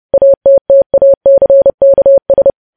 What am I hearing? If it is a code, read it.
Morse code, heard as a keyed tone, about 30 wpm, 550 Hz. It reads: ATTACKS
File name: podcast / News headlines-1x-30x30